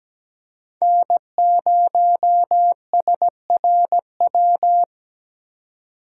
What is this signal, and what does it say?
Morse code, heard as a keyed tone, about 17 wpm, 700 Hz. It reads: N0SRW
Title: USA Call Signs - 3x Character Spacing - 17wpm